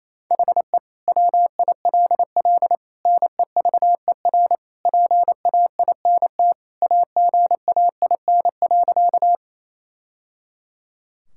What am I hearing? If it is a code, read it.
Morse code, heard as a keyed tone, about 28 wpm, 700 Hz. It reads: HE WILL NEVER PAINT AGAIN.
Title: Sentences from Top 1000 Words - Review Part 6 - 28wpm